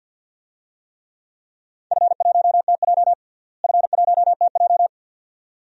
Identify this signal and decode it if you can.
Morse code, heard as a keyed tone, about 46 wpm, 700 Hz. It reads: F1TJ F1TJ
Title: DXCC-100 Call Signs - 4 Characters - Repeated 2x - 50wpm